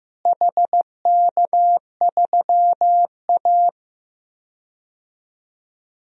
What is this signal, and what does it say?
Morse code, heard as a keyed tone, about 15 wpm, 700 Hz. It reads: HK3A
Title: DXCC-100 Call Signs - 4 Characters - Repeated 3x - 15wpm